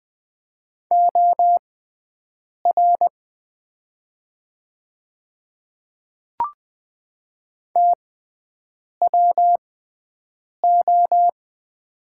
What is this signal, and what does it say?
Morse code, heard as a keyed tone, about 20 wpm, spaced as for 7 wpm, 700 Hz. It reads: OR TWO
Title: 6x Character Spacing - Top 100 Words - 20wpm